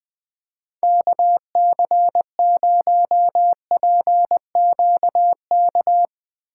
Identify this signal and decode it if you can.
Morse code, heard as a keyed tone, about 20 wpm, 700 Hz. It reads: KC0PQK
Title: Speed Racing - USA Call Signs 17wpm